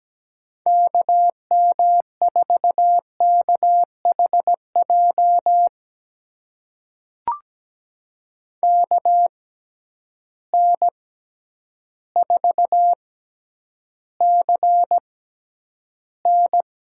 Morse code, heard as a keyed tone, about 17 wpm, spaced as for 9 wpm, 700 Hz. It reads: KM4KHJ KN4CN
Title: USA Call Signs - 6 Characters - 6x Character Spacing - 17wpm